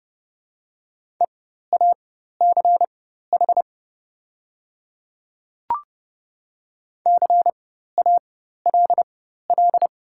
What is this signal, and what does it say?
Morse code, heard as a keyed tone, about 30 wpm, spaced as for 14 wpm, 690 Hz. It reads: EACH CALL
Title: 4x Character Spacing - Top 100 Words - 30wpm